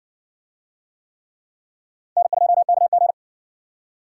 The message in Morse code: N2KG

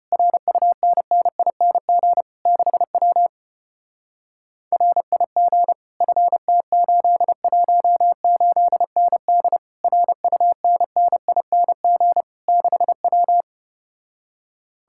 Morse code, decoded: RUNNING 6W RIG FT818ND RUNNING 6W

